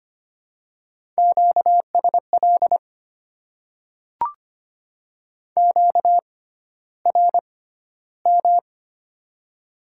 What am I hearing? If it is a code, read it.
Morse code, heard as a keyed tone, about 25 wpm, spaced as for 13 wpm, 700 Hz. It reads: QSL QRM
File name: Top 90 QSO Elements - 6x Character Spacing - 25wpm